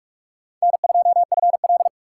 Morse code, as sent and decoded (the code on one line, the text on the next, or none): -. ..--- ..-. .-..
N2FL